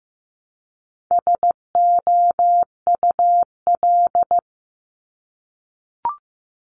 Morse code, heard as a keyed tone, about 15 wpm, 700 Hz. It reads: SOUL
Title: Multi-letter Phonemes Repeated 3x 15wpm